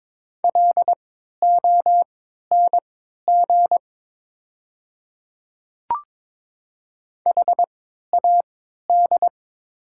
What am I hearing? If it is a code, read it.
Morse code, heard as a keyed tone, about 22 wpm, spaced as for 12 wpm, 700 Hz. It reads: LONG HAD
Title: 3x Character Spacing - Top 100 Words - 22wpm